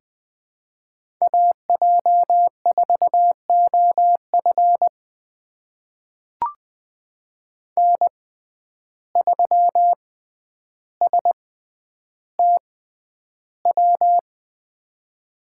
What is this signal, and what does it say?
Morse code, heard as a keyed tone, about 20 wpm, spaced as for 10 wpm, 700 Hz. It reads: AJ4OF N3STW
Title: USA Call Signs - 5 Characters - 6x Character Spacing - 20wpm